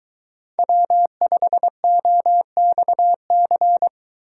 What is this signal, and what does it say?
Morse code, heard as a keyed tone, about 23 wpm, 700 Hz. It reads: W5OXC